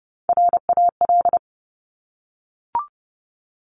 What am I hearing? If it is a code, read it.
Morse code, heard as a keyed tone, about 30 wpm, 700 Hz. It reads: RAL
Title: Top Three Letter Combinations in Words 30wpm